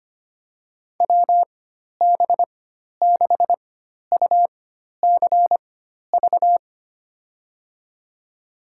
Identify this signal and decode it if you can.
Morse code, heard as a keyed tone, about 25 wpm, spaced as for 12 wpm, 700 Hz. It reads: WB6UCV
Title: USA Call Signs - 6 Characters - 4x Character Spacing - 25wpm